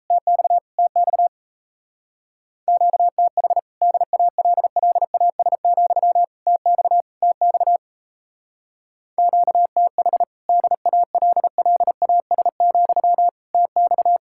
Morse code, heard as a keyed tone, about 32 wpm, 700 Hz. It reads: TX TX QTH DALLAS, TX TX QTH DALLAS, TX
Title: Speed-Racing Common QSO Phrases - 28wpm